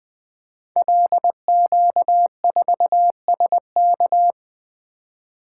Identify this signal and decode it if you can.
Morse code, heard as a keyed tone, about 20 wpm, 700 Hz. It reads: LQ4SK
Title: Code Groups - 5 Letter:Numbers - 20wpm